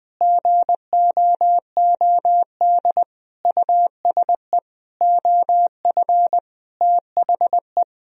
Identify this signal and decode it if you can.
Morse code, heard as a keyed tone, about 21 wpm, 700 Hz. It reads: GOOD USE OF THE
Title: Sentences from Top 1000 Words - Review Part 4 - 20wpm